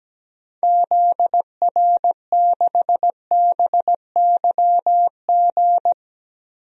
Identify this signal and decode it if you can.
Morse code, heard as a keyed tone, about 17 wpm, 700 Hz. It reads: ZR6BYG
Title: DXCC-100 Call Signs - 6 Characters - Repeated 2x - 17wpm